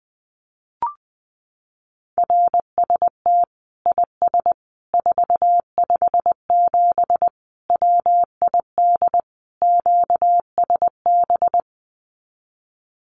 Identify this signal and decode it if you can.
Morse code, heard as a keyed tone, about 20 wpm, 700 Hz. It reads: RST IS 457 WID QSB
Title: Common QSO Phrases 20wpm